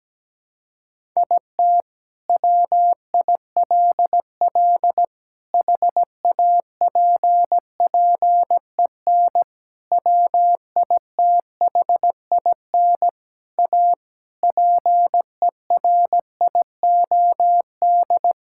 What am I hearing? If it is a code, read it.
Morse code, heard as a keyed tone, about 17 wpm, 700 Hz. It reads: IT WILL HAPPEN WITHIN A PERIOD
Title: Sentences from Top 1000 Words - Review Part 5 - 17wpm